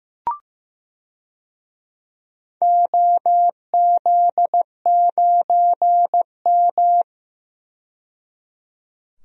Morse code, heard as a keyed tone, about 15 wpm, 700 Hz. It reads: OZ9M
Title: DXCC-100 Call Signs - 4 Characters - 15wpm